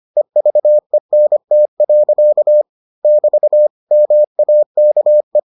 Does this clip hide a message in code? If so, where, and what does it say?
Yes: Morse code from 0.2 to 5.4 s, heard as a keyed tone, about 25 wpm, 600 Hz. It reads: EVENT. <BT> MAKE